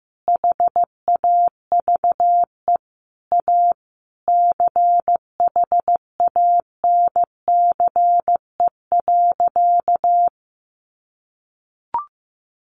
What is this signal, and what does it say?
Morse code, heard as a keyed tone, about 15 wpm, 700 Hz. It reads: HAVE A CHANCE.
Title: Sentences from Top 1400 Words 15wpm